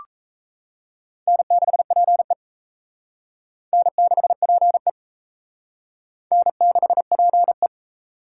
Morse code, heard as a keyed tone, about 34 wpm, 700 Hz. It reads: N6PE N6PE N6PE
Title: Speed Racing - USA Call Signs 28wpm